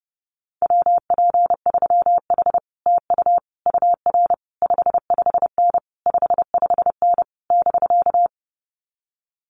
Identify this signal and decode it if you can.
Morse code, heard as a keyed tone, about 31 wpm, 700 Hz. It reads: WP3H TU UR 55N 55N <BK>
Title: Common QSO Phrases Encore 30wpm